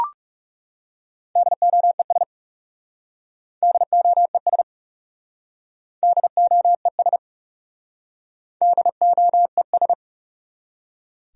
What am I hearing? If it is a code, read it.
Morse code, heard as a keyed tone, about 36 wpm, 700 Hz. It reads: DOES DOES DOES DOES
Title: Speed-Racing Top 300 Words 30wpm